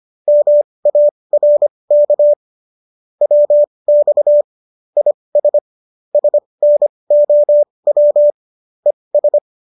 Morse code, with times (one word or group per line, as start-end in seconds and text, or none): 0.3-2.3 MARK
3.2-4.4 WX
5.0-5.6 IS
6.1-8.3 SNOW
8.9-9.4 ES